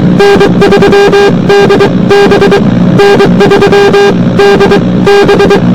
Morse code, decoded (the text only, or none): N3DB N3DB